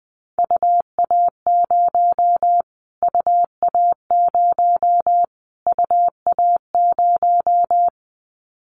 Morse code, heard as a keyed tone, about 20 wpm, 700 Hz. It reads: UA0 UA0 UA0
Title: Callsign Prefixes for Top 25 DXCC - Repeated 3x - 20wpm